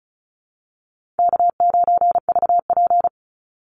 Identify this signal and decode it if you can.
Morse code, heard as a keyed tone, about 34 wpm, 700 Hz. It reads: K9VP
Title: Call Signs Repeated 2x - 35wpm